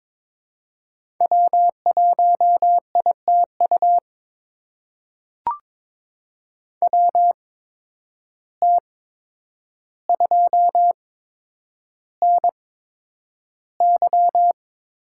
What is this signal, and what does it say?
Morse code, heard as a keyed tone, about 21 wpm, spaced as for 9 wpm, 700 Hz. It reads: W1ITU WT2NY
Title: USA Call Signs - 5 Characters - 8x Character Spacing - 22wpm